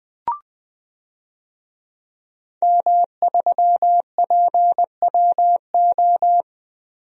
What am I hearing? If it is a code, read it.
Morse code, heard as a keyed tone, about 20 wpm, 700 Hz. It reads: M3PWO
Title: DXCC-100 Call Signs - 5 Characters - 20wpm